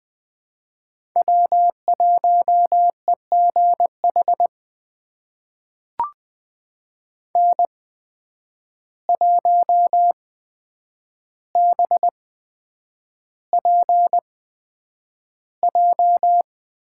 Morse code, heard as a keyed tone, about 20 wpm, spaced as for 9 wpm, 700 Hz. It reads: W1EGH N1BPJ
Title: USA Call Signs - 5 Characters - 8x Character Spacing - 20wpm